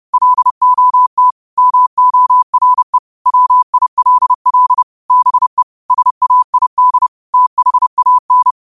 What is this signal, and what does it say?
Morse code, heard as a keyed tone, about 30 wpm, 1000 Hz. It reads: LOT MORE WILL BE SAID THAN
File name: qotdfiles / 30wpm